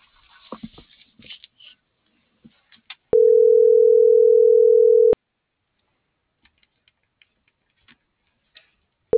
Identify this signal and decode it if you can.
Ringback tone (North America), 440 + 480 Hz.